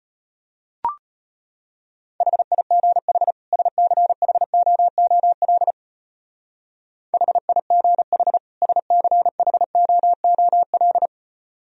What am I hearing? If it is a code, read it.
Morse code, heard as a keyed tone, about 31 wpm, 700 Hz. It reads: HIGH SCHOOL HIGH SCHOOL
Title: Speed-Racing - Sets of 2 Words 25wpm